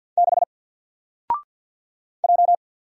Morse code, heard as a keyed tone, about 48 wpm, 700 Hz. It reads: 6 J
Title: Single Letter-Number Alt 50wpm